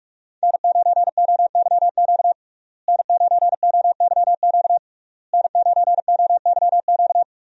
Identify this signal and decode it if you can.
Morse code, heard as a keyed tone, about 46 wpm, 700 Hz. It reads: N9OYQ N9OYQ N9OYQ
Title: USA Call Signs - 5 Characters - Repeated 3x - 45wpm